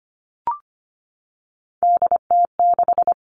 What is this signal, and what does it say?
Morse code, heard as a keyed tone, about 25 wpm, 700 Hz. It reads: DT6
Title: All Cut Numbers - 3-Digit 25wpm